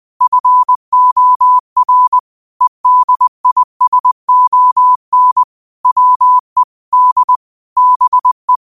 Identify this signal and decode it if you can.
Morse code, heard as a keyed tone, about 20 wpm, 1000 Hz. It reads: FOR EDISON WE D BE